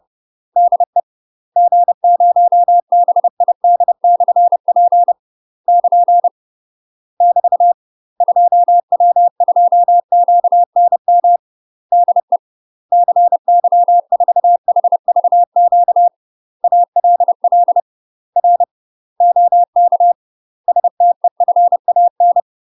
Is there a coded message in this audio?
Yes: Morse code from 0.6 to 22.4 s, heard as a keyed tone, about 30 wpm, 700 Hz. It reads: DE G0BID/P <KN> <BT> 2W2QNM DE CY4HVQ ALL R OK STEFAN